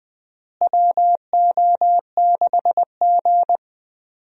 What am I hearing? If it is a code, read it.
Morse code, heard as a keyed tone, about 20 wpm, 700 Hz. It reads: WO6G
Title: USA Call Signs - 4 Characters - Repeated 3x - 20wpm